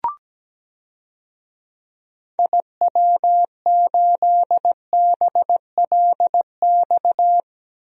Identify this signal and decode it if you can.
Morse code, heard as a keyed tone, about 17 wpm, 700 Hz. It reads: IW8BLX
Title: DXCC-100 Call Signs - 6 Characters - Repeated 3x - 17wpm